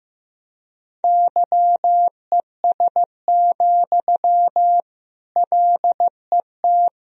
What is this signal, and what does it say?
Morse code, heard as a keyed tone, about 15 wpm, 700 Hz. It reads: YES, LET
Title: Sentences from Top 1000 Words - Review Part 1 - 15wpm